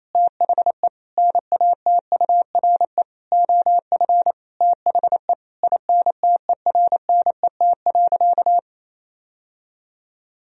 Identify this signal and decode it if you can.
Morse code, heard as a keyed tone, about 28 wpm, 700 Hz. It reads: THE NATURE OF THE INTERNET.
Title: Sentences from Top 1000 Words - Review Part 4 - 28wpm